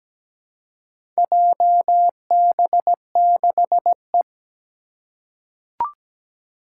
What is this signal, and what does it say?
Morse code, heard as a keyed tone, about 17 wpm, 700 Hz. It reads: JB6E
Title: Four Challenging Letter-Number Combinations 17wpm